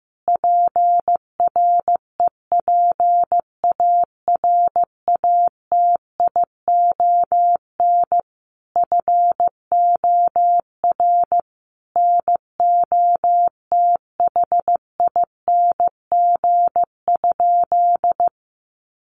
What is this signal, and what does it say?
Morse code, heard as a keyed tone, about 15 wpm, 700 Hz. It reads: PREPARATION FOR NOTHING?
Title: Sentences from Top 1400 Words 15wpm